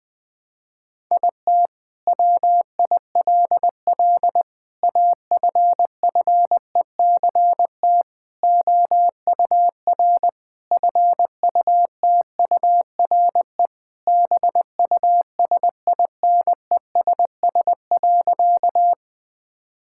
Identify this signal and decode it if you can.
Morse code, heard as a keyed tone, about 20 wpm, 700 Hz. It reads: IT WILL AFFECT OUR FUTURE BUSINESS.
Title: Sentences from Top 1000 Words - Review Part 6 - 20wpm